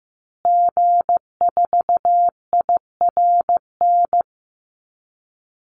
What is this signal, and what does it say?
Morse code, heard as a keyed tone, about 15 wpm, 700 Hz. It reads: G4IRN
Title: Call Signs Repeated 3x - 15wpm